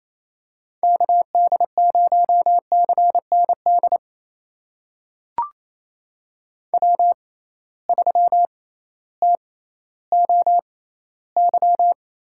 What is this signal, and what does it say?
Morse code, heard as a keyed tone, about 28 wpm, spaced as for 15 wpm, 700 Hz. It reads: KD0CND W3TOY
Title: USA Call Signs - 6x Character Spacing - 28wpm